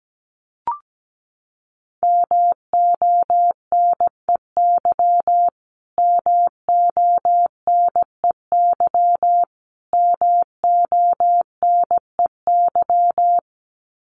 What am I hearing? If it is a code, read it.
Morse code, heard as a keyed tone, about 17 wpm, 700 Hz. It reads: MONEY MONEY MONEY
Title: Top 500 Words Repeated 17wpm